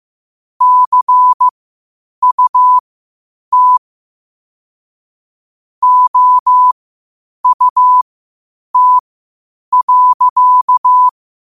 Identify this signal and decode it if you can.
Morse code, heard as a keyed tone, about 15 wpm, spaced as for 8 wpm, 1000 Hz. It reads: CUT OUT.